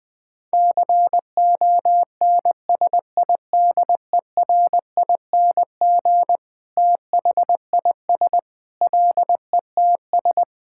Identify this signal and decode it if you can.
Morse code, heard as a keyed tone, about 21 wpm, 700 Hz. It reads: CONSIDERING THIS LETS
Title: Sentences with 9-Word Phrases - 20wpm